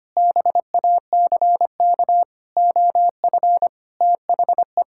Morse code, heard as a keyed tone, about 25 wpm, 700 Hz. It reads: BACK OF THE
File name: practice / 25wpm / Sentences with 3-Word Phrases - 25wpm